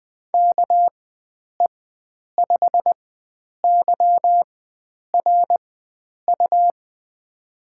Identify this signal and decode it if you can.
Morse code, heard as a keyed tone, about 20 wpm, spaced as for 9 wpm, 700 Hz. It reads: KE5YRU
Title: USA Call Signs - 6 Characters - 4x Character Spacing - 20wpm